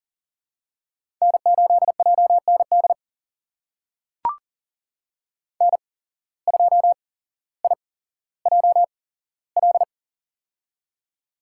Morse code, heard as a keyed tone, about 38 wpm, spaced as for 17 wpm, 700 Hz. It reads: N8JND N2IJL
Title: USA Call Signs - 5 Characters - 8x Character Spacing - 40wpm